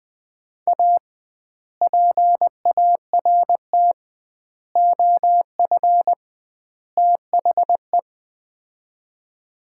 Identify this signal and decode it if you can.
Morse code, heard as a keyed tone, about 21 wpm, 700 Hz. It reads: A PART OF THE
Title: 2x Word-Spacing - Sets of 4 Words - 20wpm